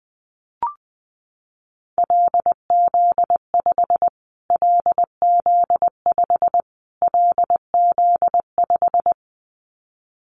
Morse code, heard as a keyed tone, about 19 wpm, 700 Hz. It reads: LZ5 LZ5 LZ5
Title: Callsign Prefixes for Top 25 DXCC - Repeated 3x - 20wpm